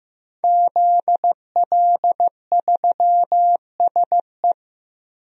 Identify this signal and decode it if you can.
Morse code, heard as a keyed tone, about 15 wpm, 700 Hz. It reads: ZL3SE
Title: DXCC-100 Call Signs - 5 Characters - Repeated 2x - 15wpm